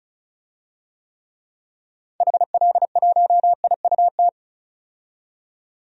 Morse code, heard as a keyed tone, about 36 wpm, 700 Hz. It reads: HL1IUT
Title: DXCC-100 Call Signs - 6 Characters - 35wpm